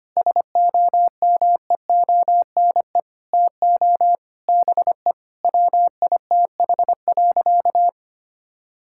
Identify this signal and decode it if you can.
Morse code, heard as a keyed tone, about 25 wpm, 700 Hz. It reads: SOMEONE TO BE WITH.
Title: Sentences from Top 1000 Words - Review Part 5 - 25wpm